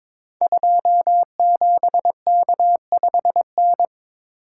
Morse code, heard as a keyed tone, about 22 wpm, 700 Hz. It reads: 27K5N
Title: Code Groups - 5 Letter:Numbers - 22wpm